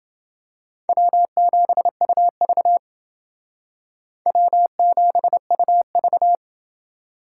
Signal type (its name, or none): Morse code